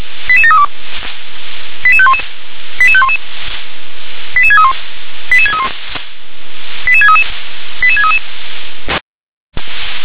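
Five-tone selective call (ZVEI), heard as quick runs of five tones, tones of about 70 ms. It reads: 88732 88511 88511 88631 88631 88633 88633
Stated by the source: ZVEI five-tone selective call received by SDR.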